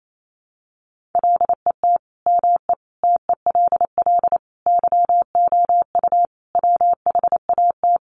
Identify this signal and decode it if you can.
Morse code, heard as a keyed tone, about 28 wpm, 700 Hz. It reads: LET ME TELL YOU WHAT
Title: Sets of 5 Words Encore 28wpm